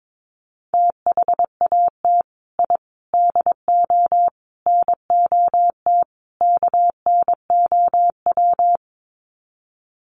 Morse code, heard as a keyed tone, about 22 wpm, 700 Hz. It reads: THAT I DO NOT KNOW
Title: Sets of 5 Words Encore2 22wpm